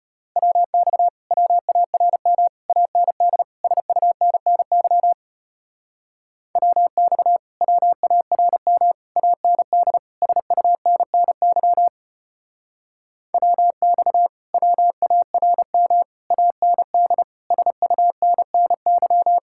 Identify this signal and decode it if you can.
Morse code, heard as a keyed tone, about 33 wpm, 700 Hz. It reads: WX WARM AND SUNNY WX WARM AND SUNNY WX WARM AND SUNNY